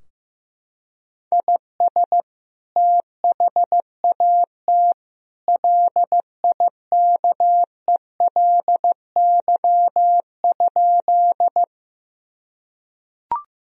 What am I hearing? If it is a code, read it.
Morse code, heard as a keyed tone, about 15 wpm, 700 Hz. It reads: IS THAT LIKELY?
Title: Sentences from Top 1000 Words - Review Part 4 - 15wpm